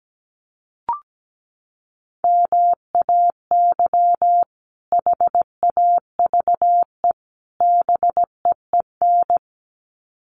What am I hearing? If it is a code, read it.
Morse code, heard as a keyed tone, about 17 wpm, 700 Hz. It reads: MAY HAVE BEEN
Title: Sets of 3 Words 17wpm